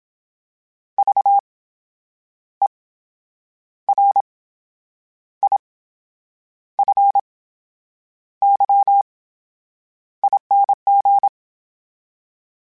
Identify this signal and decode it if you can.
Morse code, heard as a keyed tone, about 27 wpm, spaced as for 8 wpm, 800 Hz. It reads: VERIFYING
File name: Special INGs / Special INGs-mp3s / VerySlowToFaster Ing @25